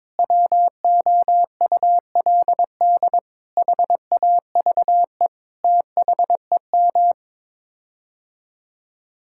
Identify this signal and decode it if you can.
Morse code, heard as a keyed tone, about 22 wpm, 700 Hz. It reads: WOULD HAVE THEM
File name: practice / 22wpm / Sets of 8 Words - 22wpm